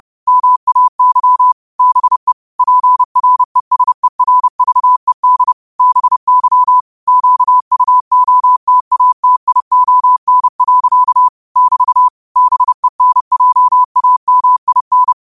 Morse code, heard as a keyed tone, about 30 wpm, 1000 Hz. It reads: MAY BE PRESERVED BY QUOTATION. <BT> BENJAMIN
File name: qotdfiles / 30wpm